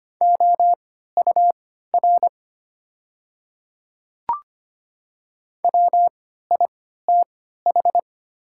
Morse code, heard as a keyed tone, about 25 wpm, spaced as for 14 wpm, 700 Hz. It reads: OUR WITH
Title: 3x Character Spacing - Top 300 Words - 25wpm